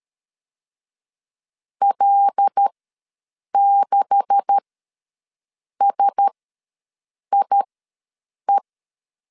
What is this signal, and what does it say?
Morse code, heard as a keyed tone, about 13 wpm, spaced as for 6 wpm, 800 Hz. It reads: L6SIE
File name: LEZIONI / 10 SIGNS / 10B INT ws 12wpm 3527